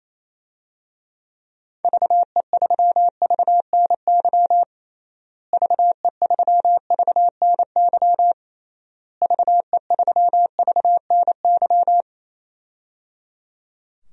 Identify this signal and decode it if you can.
Morse code, heard as a keyed tone, about 28 wpm, 700 Hz. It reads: VE3VNY VE3VNY VE3VNY
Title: DXCC-100 Call Signs - 6 Characters - Repeated 3x - 28wpm